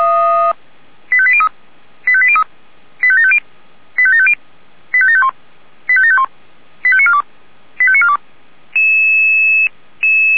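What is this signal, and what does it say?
Five-tone selective call (ZVEI), heard as quick runs of five tones, tones of about 70 ms. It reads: 85803 85803 85750 85750 85751 85751 85942 85942
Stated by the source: ZVEI five-tone call received by SDR.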